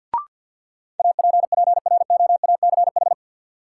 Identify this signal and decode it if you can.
Morse code, heard as a keyed tone, about 48 wpm, 700 Hz. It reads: APPROACH
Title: Sentences from Top 1000 Words - Review Part 6 - New Word Drill - 50wpm